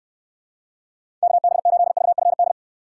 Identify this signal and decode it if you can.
Morse code, heard as a keyed tone, about 69 wpm, 700 Hz. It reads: KB8VFD